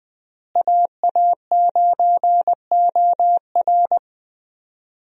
Morse code, heard as a keyed tone, about 20 wpm, 700 Hz. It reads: AA9OR
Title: USA Call Signs - 5 Characters - Repeated 3x - 20wpm